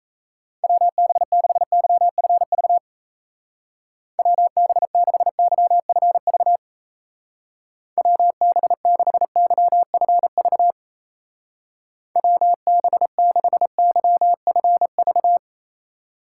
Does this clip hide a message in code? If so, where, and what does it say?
Yes: Morse code from 0.6 to 15.4 s, heard as a keyed tone, about 35 wpm, 700 Hz. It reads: WB6YFV WB6YFV WB6YFV WB6YFV